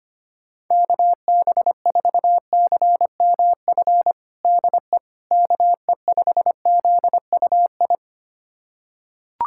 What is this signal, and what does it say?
Morse code, heard as a keyed tone, about 25 wpm, 700 Hz. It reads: KB4CMF DE KE5ZUI
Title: USA Callsign DE Callsign 25wpm